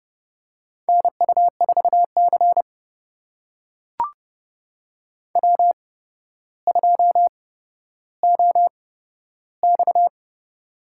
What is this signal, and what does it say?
Morse code, heard as a keyed tone, about 30 wpm, spaced as for 13 wpm, 700 Hz. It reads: NU4C W2OX
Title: USA Call Signs - 4 Characters - 8x Character Spacing - 30wpm